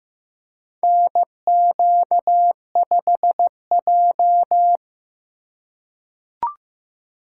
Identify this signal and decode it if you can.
Morse code, heard as a keyed tone, about 15 wpm, 700 Hz. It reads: NQ5J
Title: USA Call Signs - 2x Character Spacing - 15wpm